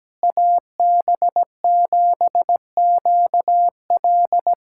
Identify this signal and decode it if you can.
Morse code, heard as a keyed tone, about 17 wpm, 700 Hz. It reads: AB7QL
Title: USA Callsign DE Callsign - 5 Characters - 17wpm